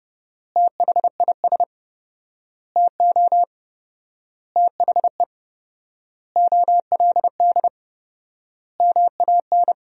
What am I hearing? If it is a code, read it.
Morse code, heard as a keyed tone, about 30 wpm, 700 Hz. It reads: THIS TO THE OLD MAN
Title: 4x Word-Spacing - Sentences from Top 200 Words - 30wpm